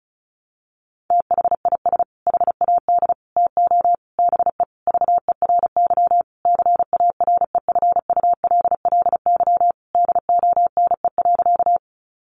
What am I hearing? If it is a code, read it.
Morse code, heard as a keyed tone, about 34 wpm, 700 Hz. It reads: THIS HAD TO BE VERY CAREFULLY DONE.
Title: Sentences from Top 1300 Words 35wpm